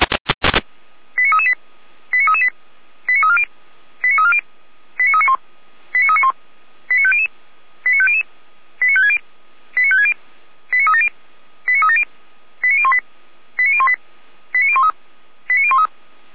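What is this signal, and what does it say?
Five-tone selective call (ZVEI), heard as quick runs of five tones, tones of about 70 ms. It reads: 89338 89338 89340 89340 89391 89391 89500 89500 89560 89560 89370 89370 89018 89018 89013 89013
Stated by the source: ZVEI five-tone selective call received with an SDR and recorded by Feuerwehr Orlamünde.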